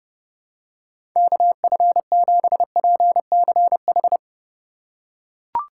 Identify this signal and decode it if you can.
Morse code, heard as a keyed tone, about 29 wpm, 700 Hz. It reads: KF7PCH
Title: USA Call Signs - 4x Character Spacing - 30wpm